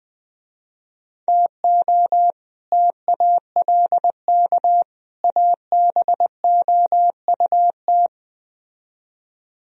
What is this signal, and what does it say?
Morse code, heard as a keyed tone, about 20 wpm, 700 Hz. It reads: TO TALK ABOUT
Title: Speed-Racing - Sets of 3 Words 17wpm